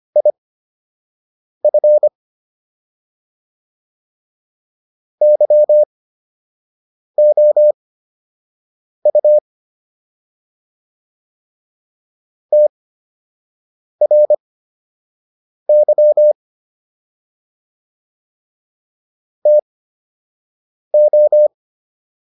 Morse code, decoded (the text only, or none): IF YOU TRY TO